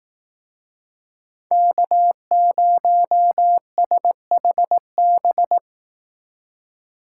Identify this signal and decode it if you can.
Morse code, heard as a keyed tone, about 18 wpm, 700 Hz. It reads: K0SHB
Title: Speed Racing - USA 1x3 Call Signs 15wpm